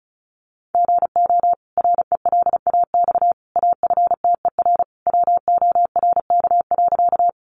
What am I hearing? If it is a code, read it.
Morse code, heard as a keyed tone, about 34 wpm, 700 Hz. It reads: GO RELAX AFTER WORK.
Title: Sentences from Top 1400 Words 35wpm